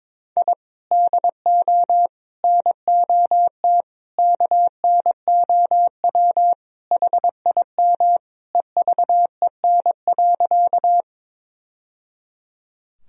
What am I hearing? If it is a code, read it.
Morse code, heard as a keyed tone, about 22 wpm, 700 Hz. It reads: I DO NOT KNOW HIM EVEN.
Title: Sentences from Top 1000 Words - Review Part 1 - 22wpm